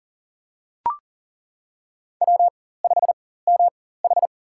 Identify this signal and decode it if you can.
Morse code, heard as a keyed tone, about 38 wpm, spaced as for 18 wpm, 700 Hz. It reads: W5MH
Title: USA Call Signs - 4 Characters - 4x Character Spacing - 40wpm